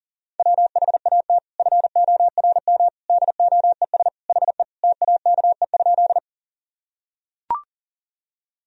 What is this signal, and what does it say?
Morse code, heard as a keyed tone, about 40 wpm, 700 Hz. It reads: WHAT FORM DOES HE TAKE?
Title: Sentences from Top 1000 Words - Review Part 1 - 40wpm